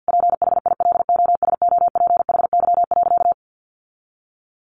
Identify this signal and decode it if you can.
Morse code, heard as a keyed tone, about 50 wpm, 700 Hz. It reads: PHILOSOPHY.
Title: Sentences from Top 900 Words 50wpm